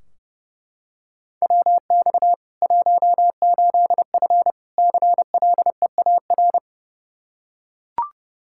Morse code, heard as a keyed tone, about 30 wpm, 700 Hz. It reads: WX 18F CLEAR
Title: Practice QSO - Exchange 2 - Info - 30wpm